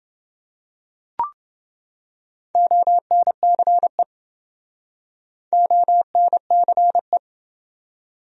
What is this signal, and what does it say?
Morse code, heard as a keyed tone, about 28 wpm, 700 Hz. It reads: ONCE ONCE